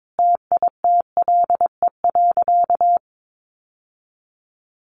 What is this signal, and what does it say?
Morse code, heard as a keyed tone, about 22 wpm, 700 Hz. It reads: TITLE.
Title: Sentences from Top 1200 Words 22wpm